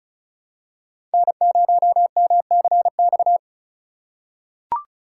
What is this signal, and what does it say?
Morse code, heard as a keyed tone, about 36 wpm, 700 Hz. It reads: N0MCX
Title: USA Call Signs - 5 Characters - 3x Character Spacing - 35wpm